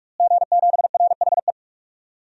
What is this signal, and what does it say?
Morse code, heard as a keyed tone, about 44 wpm, 700 Hz. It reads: G7RHE
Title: DXCC-100 Call Signs - 5 Characters - Repeated 3x - 45wpm